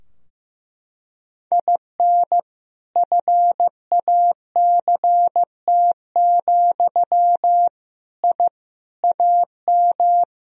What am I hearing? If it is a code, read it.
Morse code, heard as a keyed tone, about 16 wpm, 700 Hz. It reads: IN FACT, I AM
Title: Sentences from Top 1000 Words - Review Part 3 - 15wpm